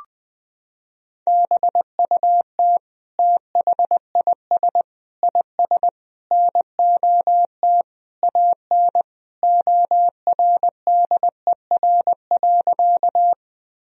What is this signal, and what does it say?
Morse code, heard as a keyed tone, about 20 wpm, 700 Hz. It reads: BUT THIS IS NOT AN ORDER.
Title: Sentences from Top 1000 Words - Review Part 2 - 20wpm